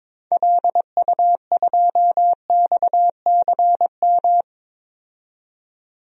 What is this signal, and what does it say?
Morse code, heard as a keyed tone, about 22 wpm, 700 Hz. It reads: LU2XCM
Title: DXCC-100 Call Signs - 6 Characters - 22wpm